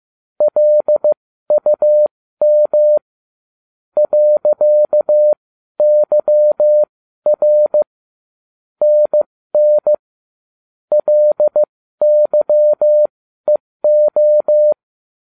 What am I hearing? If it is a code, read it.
Morse code, heard as a keyed tone, about 15 wpm, spaced as for 12 wpm, 600 Hz. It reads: LUM .YR NN LYEO